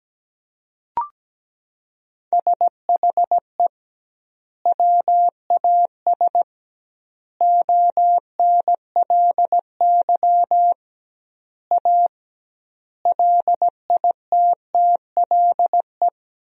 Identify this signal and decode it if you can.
Morse code, heard as a keyed tone, about 17 wpm, 700 Hz. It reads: SHE WAS ONLY A LITTLE
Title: 2x Word-Spacing - Sentences from Top 400 Words - 17wpm